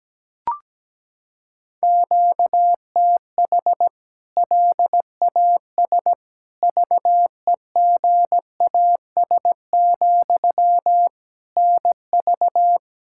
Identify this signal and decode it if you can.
Morse code, heard as a keyed tone, about 17 wpm, 700 Hz. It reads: QTH LAS VEGAS, NV
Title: K1USN SST QSO Practice 17wpm